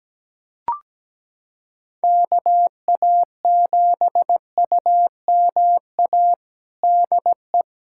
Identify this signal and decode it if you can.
Morse code, heard as a keyed tone, about 17 wpm, 700 Hz. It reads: KA7UMA DE
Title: USA Callsign DE Callsign - 6 Characters - 17wpm